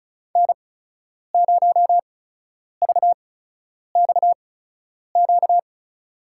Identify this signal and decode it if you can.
Morse code, heard as a keyed tone, about 36 wpm, spaced as for 10 wpm, 700 Hz. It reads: N0VXQ
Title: USA Call Signs - 5 Characters - 8x Character Spacing - 35wpm